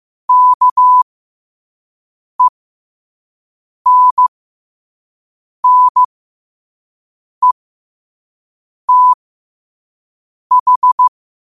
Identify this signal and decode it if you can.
Morse code, heard as a keyed tone, about 15 wpm, spaced as for 5 wpm, 1000 Hz. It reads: KENNETH
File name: qotdfiles / 5wpm